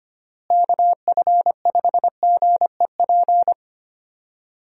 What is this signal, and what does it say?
Morse code, heard as a keyed tone, about 25 wpm, 700 Hz. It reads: KF5GEP
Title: USA Call Signs - 6 Characters - Repeated 2x - 25wpm